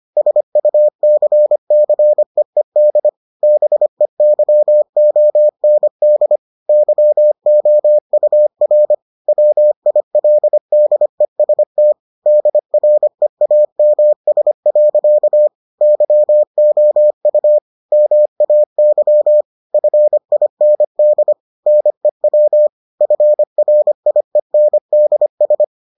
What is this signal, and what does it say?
Morse code, heard as a keyed tone, about 25 wpm, 600 Hz. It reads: SUCCEED BEYOND YOUR WILDEST DREAMS. YOU MAY FIND NEW FRIENDS